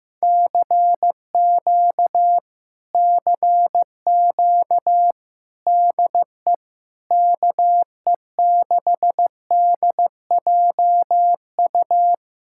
Morse code, heard as a keyed tone, about 15 wpm, 700 Hz. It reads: CQ CQ DE KE6DJU
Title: Speed-Racing Common QSO Phrases Encore - 15wpm